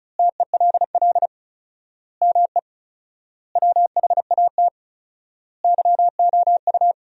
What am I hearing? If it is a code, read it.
Morse code, heard as a keyed tone, about 36 wpm, 700 Hz. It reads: TELL ME WHAT YOU
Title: 4x Word-Spacing - Sentences from Top 200 Words - 35wpm